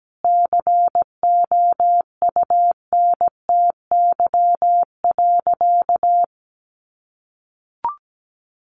Morse code, heard as a keyed tone, about 17 wpm, 700 Hz. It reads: COUNTY.
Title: Sentences from Top 1200 Words 17wpm